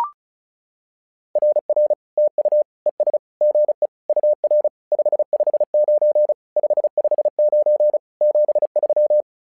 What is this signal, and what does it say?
Morse code, heard as a keyed tone, about 34 wpm, 600 Hz. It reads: RR TU ES GE UR 559 559 73
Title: SOTA QSO Practice 35wpm